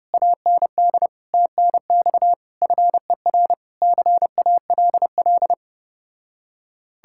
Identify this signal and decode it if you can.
Morse code, heard as a keyed tone, about 32 wpm, 700 Hz. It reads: AND TNX FER CALL